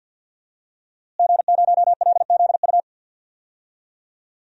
Morse code, heard as a keyed tone, about 52 wpm, 700 Hz. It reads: G0LZU
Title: DXCC-100 Call Signs - 5 Characters - Repeated 3x - 50wpm